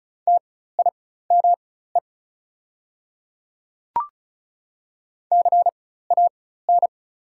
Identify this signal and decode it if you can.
Morse code, heard as a keyed tone, about 35 wpm, spaced as for 16 wpm, 690 Hz. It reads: TIME CAN
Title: 4x Character Spacing - Top 100 Words - 35wpm